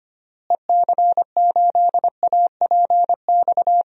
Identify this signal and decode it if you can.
Morse code, heard as a keyed tone, about 25 wpm, 700 Hz. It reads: EC8APX
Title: DXCC-100 Call Signs - 6 Characters - Repeated 3x - 25wpm